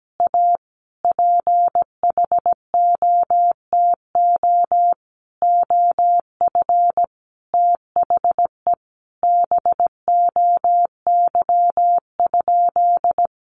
Morse code, heard as a keyed tone, about 17 wpm, 700 Hz. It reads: A PHOTO OF THE BOY?
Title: Sentences from Top 1200 Words 17wpm